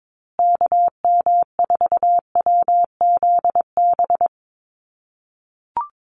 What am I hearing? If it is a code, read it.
Morse code, heard as a keyed tone, about 22 wpm, 700 Hz. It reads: KM4WZB
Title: USA Call Signs Repeated 3x 22wpm